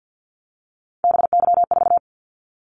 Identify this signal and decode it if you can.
Morse code, heard as a keyed tone, about 48 wpm, 700 Hz. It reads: BY4
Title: Callsign Prefixes for Top 100 DXCC - 50wpm